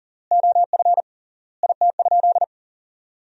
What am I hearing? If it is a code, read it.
Morse code, heard as a keyed tone, about 40 wpm, 700 Hz. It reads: OF IT?
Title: 3x Word-Spacing - Sentences from Top 100 Words - 40wpm